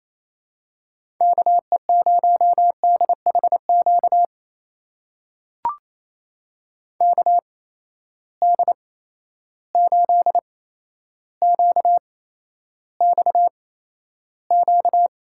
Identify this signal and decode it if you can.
Morse code, heard as a keyed tone, about 27 wpm, spaced as for 12 wpm, 700 Hz. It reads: KE0DHQ KD8QXQ
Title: USA Call Signs - 6 Characters - 8x Character Spacing - 28wpm